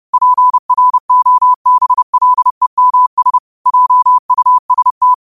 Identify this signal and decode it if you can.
Morse code, heard as a keyed tone, about 30 wpm, 1000 Hz. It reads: PROBLEMS JUST
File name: qotdfiles / 30wpm